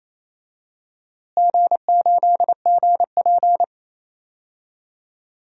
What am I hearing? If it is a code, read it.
Morse code, heard as a keyed tone, about 27 wpm, 700 Hz. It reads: G8GP
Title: DXCC-100 Call Signs - 4 Characters - Repeated 2x - 28wpm